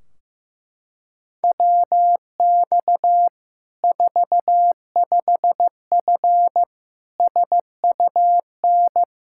Morse code, heard as a keyed tone, about 16 wpm, 700 Hz. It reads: WX 45F SUN
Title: K1USN SST QSO Practice 15wpm